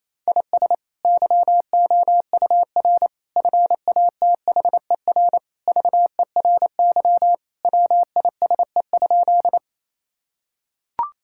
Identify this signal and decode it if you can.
Morse code, heard as a keyed tone, about 28 wpm, 700 Hz. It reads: IS YOUR FATHER VERY WISE?
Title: Sentences from Top 1600 Words 28wpm